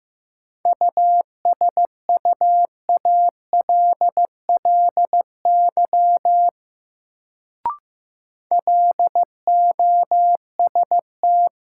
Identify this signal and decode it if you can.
Morse code, heard as a keyed tone, about 15 wpm, 700 Hz. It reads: USUALLY LOST
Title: Sentences from Top 1400 Words - New Word Drill Alt - 15wpm